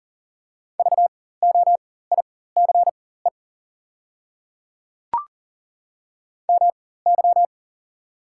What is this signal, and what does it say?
Morse code, heard as a keyed tone, about 37 wpm, spaced as for 18 wpm, 690 Hz. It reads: VOICE MY
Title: 4x Character Spacing - Top 500 Words - 40wpm